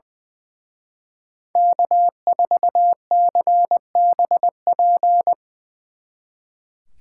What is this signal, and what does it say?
Morse code, heard as a keyed tone, about 23 wpm, 700 Hz. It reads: K4CBP
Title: Speed Racing - USA 1x3 Call Signs 20wpm